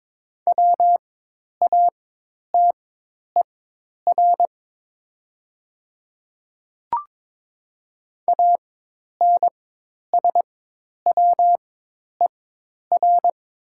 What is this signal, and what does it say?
Morse code, heard as a keyed tone, about 22 wpm, spaced as for 10 wpm, 700 Hz. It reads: WATER ANSWER
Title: 4x Character Spacing - Top 300 Words - 22wpm